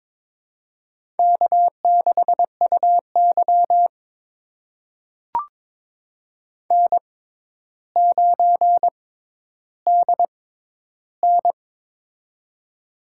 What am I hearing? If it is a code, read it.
Morse code, heard as a keyed tone, about 22 wpm, spaced as for 11 wpm, 700 Hz. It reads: K6UY N9DN
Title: USA Call Signs - 4 Characters - 6x Character Spacing - 22wpm